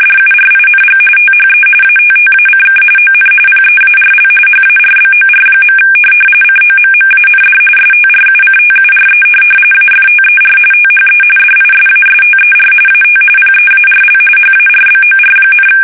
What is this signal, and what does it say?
RTTY, about 45.45 baud. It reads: DURING THE ATTACK ON PEARL HARBOR. COMMAND STRUCTURE AFTER THE DISASTER AT PEARL HARBOR,
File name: fsk850